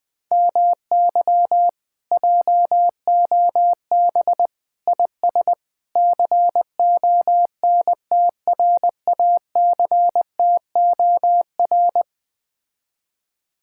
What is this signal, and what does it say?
Morse code, heard as a keyed tone, about 20 wpm, 700 Hz. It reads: MY JOB IS CONTRACTOR
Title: Speed-Racing Common QSO Phrases - 17wpm